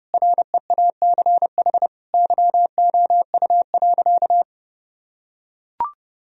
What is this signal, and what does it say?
Morse code, heard as a keyed tone, about 30 wpm, 700 Hz. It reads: REACH YOU.
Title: Sentences from Top 1000 Words - Review Part 2 - 30wpm